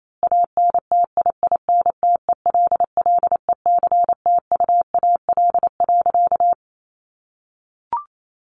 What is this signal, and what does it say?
Morse code, heard as a keyed tone, about 28 wpm, 700 Hz. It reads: ANTIINTELLECTUAL.
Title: A Sound Of Thunder 28wpm